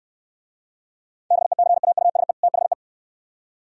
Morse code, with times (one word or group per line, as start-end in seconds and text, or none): 1.3-2.3 BECAUSE
2.4-2.7 THE